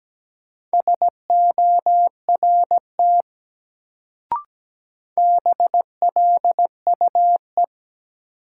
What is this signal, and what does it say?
Morse code, heard as a keyed tone, about 17 wpm, 700 Hz. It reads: SORT BLUE
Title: Sentences from Top 1000 Words - Review Part 3 - New Word Drill - 17wpm